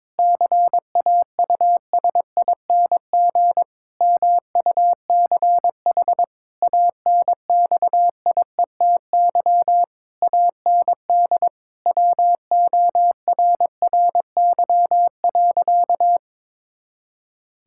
Morse code, heard as a keyed tone, about 22 wpm, 700 Hz. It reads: CAUSING MUCH ANXIETY AND WORRY.
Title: Pangram Challenge v2 - 22wpm